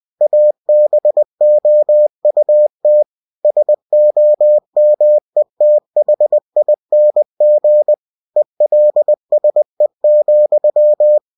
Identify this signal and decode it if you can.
Morse code, heard as a keyed tone, about 20 wpm, 600 Hz. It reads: ABOUT SOMETHING ELSE,
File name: cw-books / winnie-the-pooh / winnie-20wpm-page-0033